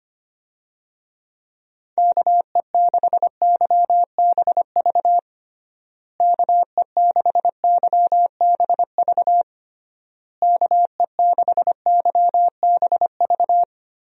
Morse code, heard as a keyed tone, about 25 wpm, 700 Hz. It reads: KE6YBV KE6YBV KE6YBV
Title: USA Call Signs - 6 Characters - Repeated 3x - 25wpm